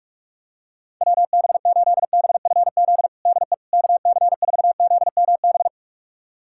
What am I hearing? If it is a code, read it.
Morse code, heard as a keyed tone, about 46 wpm, 700 Hz. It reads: WB8BFZ DE KC4ZMB
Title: USA Callsign DE Callsign - 6 Characters - 45wpm